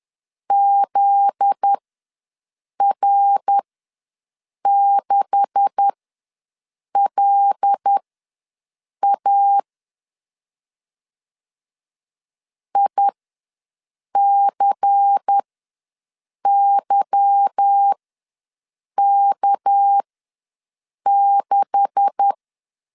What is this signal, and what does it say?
Morse code, heard as a keyed tone, about 10 wpm, spaced as for 6 wpm, 800 Hz. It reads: ZR6LA ICYK6
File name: LEZIONI / 9 286 ws 10wpm 4032